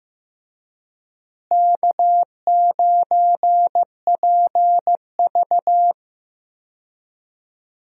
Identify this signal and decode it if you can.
Morse code, heard as a keyed tone, about 16 wpm, 700 Hz. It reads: K9PV
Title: USA Call Signs - 4 Characters - Repeated 2x - 15wpm